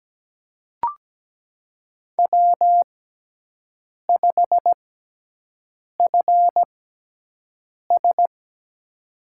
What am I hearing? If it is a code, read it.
Morse code, heard as a keyed tone, about 17 wpm, spaced as for 6 wpm, 700 Hz. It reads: W5FS
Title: USA Call Signs - 6x Character Spacing - 17wpm